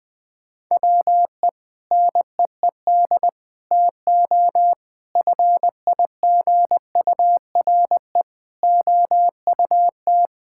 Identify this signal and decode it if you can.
Morse code, heard as a keyed tone, about 21 wpm, 700 Hz. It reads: WE NEED TO FIGURE OUT
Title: Sentences with 5-Word Phrases - 20wpm